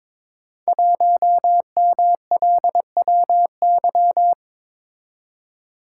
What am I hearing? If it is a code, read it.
Morse code, heard as a keyed tone, about 23 wpm, 700 Hz. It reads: 1MLWY
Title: Code Groups - 5 Letter:Numbers - 22wpm